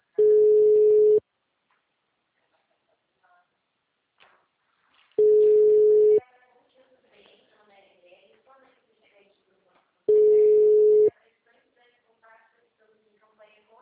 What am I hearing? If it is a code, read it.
Ringback tone (Europe, 425 Hz), with speech.
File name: out- -2007- - - .17